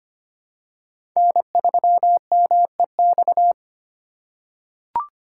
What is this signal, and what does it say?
Morse code, heard as a keyed tone, about 25 wpm, 700 Hz. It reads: N3MEX
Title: USA Call Signs - 5 Characters - 2x Character Spacing - 25wpm